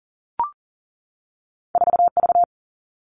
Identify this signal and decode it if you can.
Morse code, heard as a keyed tone, about 40 wpm, 700 Hz. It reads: 4V
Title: All Cut Numbers - 2-Digit 40wpm